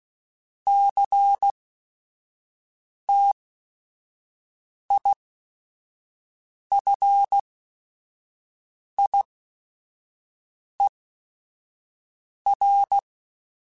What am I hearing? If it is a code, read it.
Morse code, heard as a keyed tone, about 16 wpm, spaced as for 5 wpm, 800 Hz. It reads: CTIFIER
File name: Technical Subjects / Technical Subjects-mp3s / Sw Pwr Sup @5